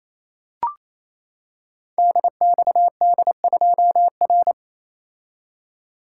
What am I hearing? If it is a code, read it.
Morse code, heard as a keyed tone, about 28 wpm, 700 Hz. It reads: DXD2R
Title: Code Groups - 5 Letter:Numbers - 28wpm